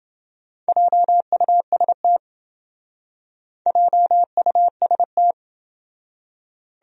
Morse code, heard as a keyed tone, about 28 wpm, 700 Hz. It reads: JUST JUST